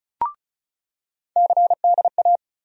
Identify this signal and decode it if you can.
Morse code, heard as a keyed tone, about 36 wpm, 700 Hz. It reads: CDA
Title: USA Call Sign Postfixes - 3 Letters 35wpm